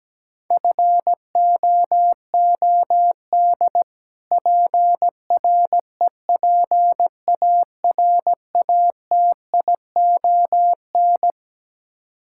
Morse code, heard as a keyed tone, about 17 wpm, 700 Hz. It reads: FOOD PREPARATION
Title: Practice QSO - Exchange 3 - More Info - 17wpm